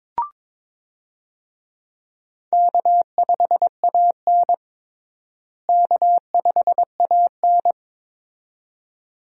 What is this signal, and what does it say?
Morse code, heard as a keyed tone, about 22 wpm, 700 Hz. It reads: K5AN K5AN
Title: USA Call Signs - 4 Characters - Repeated 2x - 22wpm